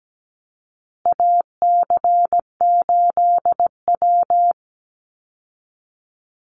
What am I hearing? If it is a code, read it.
Morse code, heard as a keyed tone, about 17 wpm, 700 Hz. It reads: AC8W
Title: Call Signs Repeated 2x - 17wpm